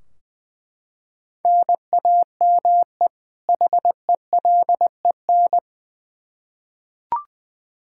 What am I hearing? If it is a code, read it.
Morse code, heard as a keyed tone, about 20 wpm, 700 Hz. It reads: NAME HELEN
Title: Practice QSO - Exchange 1 - Beginning - 20wpm